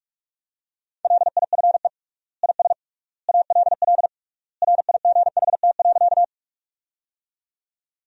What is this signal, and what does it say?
Morse code, heard as a keyed tone, about 46 wpm, 700 Hz. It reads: LIFE IS ALL RIGHT.
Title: 3x Word-Spacing - Sentences from Top 300 Words - 45wpm